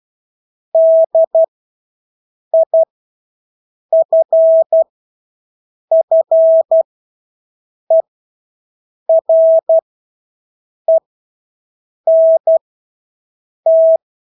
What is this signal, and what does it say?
Morse code, heard as a keyed tone, about 12 wpm, spaced as for 6 wpm, 650 Hz. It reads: DIFFERENT
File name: Engelske ord 12-6 wpm